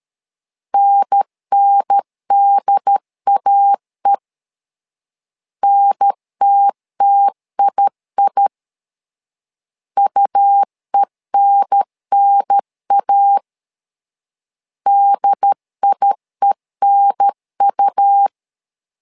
Morse code, heard as a keyed tone, about 13 wpm, 800 Hz. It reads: NNDAE NTTII UENNA DIENU
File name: LEZIONI / 2 UHDN / 2 UHDN 12wpm 1456